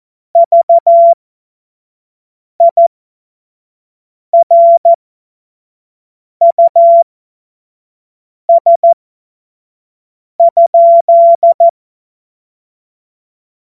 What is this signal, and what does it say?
Morse code, heard as a keyed tone, about 14 wpm, spaced as for 5 wpm, 680 Hz. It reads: VIRUS?
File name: Morsekurs / Witze / Witz 13. Juli 2021, 25 F70